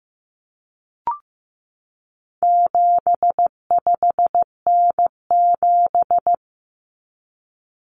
Morse code, heard as a keyed tone, about 15 wpm, 700 Hz. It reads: 75N7
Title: Common Cut Numbers - 4-Digit 15wpm